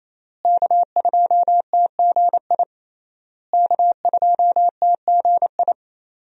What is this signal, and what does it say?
Morse code, heard as a keyed tone, about 27 wpm, 700 Hz. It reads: K2TGI K2TGI
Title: USA Call Signs - 5 Characters - Repeated 3x - 28wpm